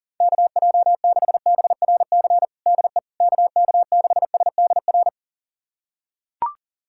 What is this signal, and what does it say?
Morse code, heard as a keyed tone, about 40 wpm, 700 Hz. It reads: KJ6BRC DE KK6SDR
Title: USA Callsign DE Callsign - 6 Characters - 40wpm